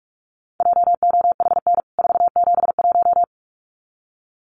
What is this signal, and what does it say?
Morse code, heard as a keyed tone, about 44 wpm, 700 Hz. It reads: JOHN 471
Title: CWT Exchange 45wpm